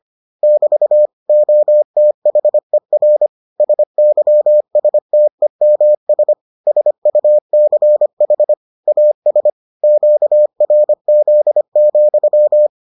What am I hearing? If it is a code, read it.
Morse code, heard as a keyed tone, about 25 wpm, 600 Hz. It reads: <BT> OTHER SYSTEMS SUCH AS QRZ,